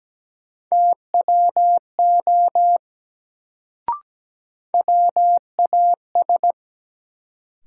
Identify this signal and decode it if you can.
Morse code, heard as a keyed tone, about 17 wpm, 700 Hz. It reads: TWO WAS
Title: Sentences from Top 100 Words - New Word Drill - 17wpm